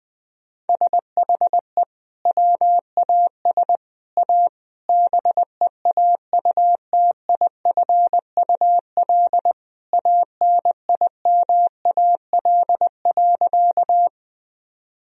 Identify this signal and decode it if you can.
Morse code, heard as a keyed tone, about 20 wpm, 700 Hz. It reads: SHE WAS A BEAUTIFUL ANIMAL.
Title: Sentences from Top 1000 Words - Review Part 1 - 20wpm